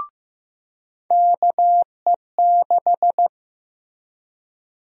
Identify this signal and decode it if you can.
Morse code, heard as a keyed tone, about 15 wpm, 700 Hz. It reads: KE6
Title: USA Call Sign Prefixes - Repeated 2x - 15wpm